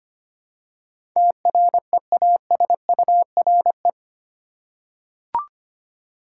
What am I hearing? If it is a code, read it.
Morse code, heard as a keyed tone, about 25 wpm, 700 Hz. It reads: TREASURE
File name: 8 Letter Words - First 3 Letters Given - 25wpm